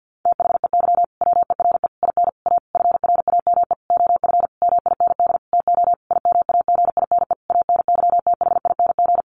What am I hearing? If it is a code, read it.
Morse code, heard as a keyed tone, about 48 wpm, 700 Hz. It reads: THEY WERE IN A FRAME OF MIND TO IMAGINE ANYTHING
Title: Sentences from Top 1400 Words 50wpm